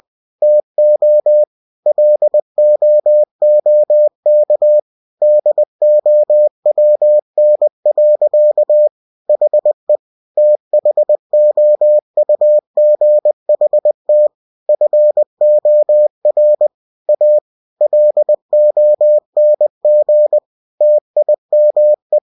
Morse code, heard as a keyed tone, about 20 wpm, 600 Hz. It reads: TO LOOK DOWN. HE THOUGHT FOR A LONG TIME
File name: cw-books / winnie-the-pooh / winnie-20wpm-page-0036